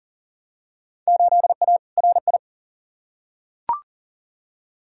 Morse code, heard as a keyed tone, about 40 wpm, 700 Hz. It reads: 8A RI
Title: Field Day Exchanges Encore - 40wpm